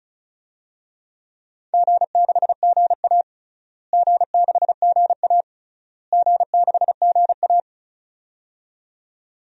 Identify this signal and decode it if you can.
Morse code, heard as a keyed tone, about 35 wpm, 700 Hz. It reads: G6GA G6GA G6GA
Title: DXCC-100 Call Signs - 4 Characters - Repeated 3x - 35wpm